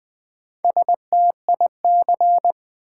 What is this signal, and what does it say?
Morse code, heard as a keyed tone, about 20 wpm, 700 Hz. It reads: STIC
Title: Build A Word - 2 Letters At A Time - 6-Letter Words - 20wpm